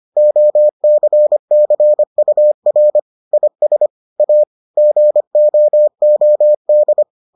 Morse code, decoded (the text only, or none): OCCUR IS A GOOD